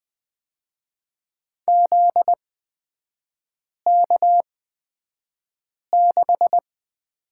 Morse code, heard as a keyed tone, about 20 wpm, spaced as for 5 wpm, 700 Hz. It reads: ZK6
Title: 243 Lesson 39e - Letter z - Three Characters 20wpm